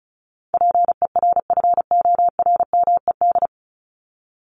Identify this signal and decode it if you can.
Morse code, heard as a keyed tone, about 34 wpm, 700 Hz. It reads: PERFORMED
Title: ED Suffix - Top 100 Words with Base Words 35wpm